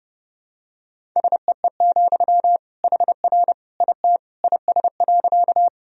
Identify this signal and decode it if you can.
Morse code, heard as a keyed tone, about 30 wpm, 700 Hz. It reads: SEE, HR IT IS.
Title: Sentences from Top 200 Words and Abbreviations 30wpm